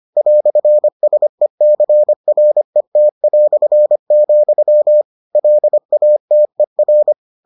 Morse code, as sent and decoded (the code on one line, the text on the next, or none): .-..-. ... -.-. .-. - .-..-. --..-- / .-.. .- - . .-.
"SECRET", LATER